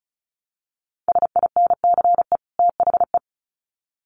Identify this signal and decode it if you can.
Morse code, heard as a keyed tone, about 34 wpm, 700 Hz. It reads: SINCE THE
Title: Sets of 2 Words 35wpm